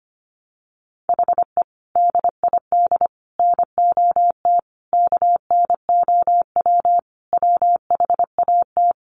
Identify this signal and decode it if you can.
Morse code, heard as a keyed tone, about 25 wpm, 700 Hz. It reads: HE DID NOT KNOW WHAT
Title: Sentences from Top 600 Words 25wpm